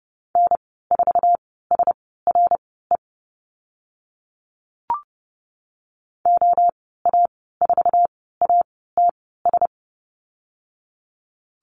Morse code, heard as a keyed tone, about 30 wpm, spaced as for 17 wpm, 700 Hz. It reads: N4SRE OA4ATS
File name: Lesson 15g - Number 5 - Call Signs 30wpm